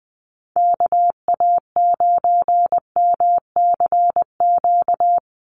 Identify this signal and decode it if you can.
Morse code, heard as a keyed tone, about 20 wpm, 700 Hz. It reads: KA9MCQ
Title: USA Call Signs 20wpm